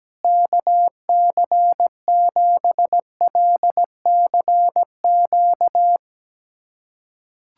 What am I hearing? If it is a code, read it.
Morse code, heard as a keyed tone, about 17 wpm, 700 Hz. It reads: KC7LCQ
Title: USA Callsign DE Callsign - 6 Characters - 17wpm